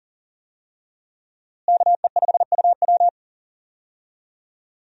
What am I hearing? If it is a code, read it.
Morse code, heard as a keyed tone, about 40 wpm, 700 Hz. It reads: KE5UW
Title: USA Call Signs - 5 Characters - Repeated 2x - 40wpm